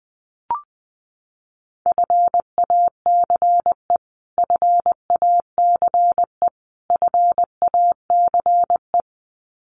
Morse code, heard as a keyed tone, about 20 wpm, 700 Hz. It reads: FACE FACE FACE
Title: Top 500 Words Repeated 20wpm